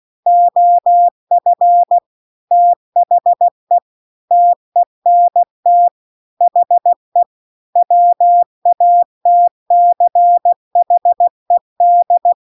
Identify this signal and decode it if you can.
Morse code, heard as a keyed tone, about 16 wpm, 700 Hz. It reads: OF THE TENT HE WATCHED